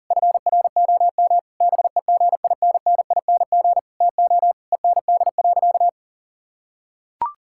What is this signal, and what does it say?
Morse code, heard as a keyed tone, about 40 wpm, 700 Hz. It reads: FROM BEGINNING TO END.
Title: Sentences from Top 1000 Words - Review Part 5 - 40wpm